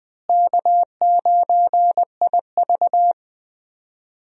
Morse code, heard as a keyed tone, about 20 wpm, 700 Hz. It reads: K9IV
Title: USA Call Signs - 4 Characters - Repeated 3x - 20wpm